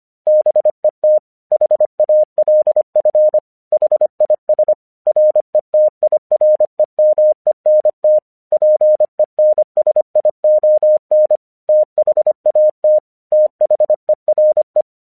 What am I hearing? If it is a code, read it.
Morse code, heard as a keyed tone, about 25 wpm, 610 Hz. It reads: BET HALF HIS RETIREMENT PENSION THAT THERE
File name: 1 English 25wpm / AQC32